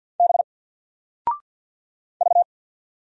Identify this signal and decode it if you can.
Morse code, heard as a keyed tone, about 53 wpm, 700 Hz. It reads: B V
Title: Single Letter-Number Alt 50wpm